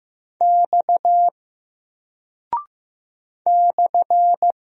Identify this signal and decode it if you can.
Morse code, heard as a keyed tone, about 15 wpm, 700 Hz. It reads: X /
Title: Single Character (Letter:Number:Punctuation:Prosign) 15wpm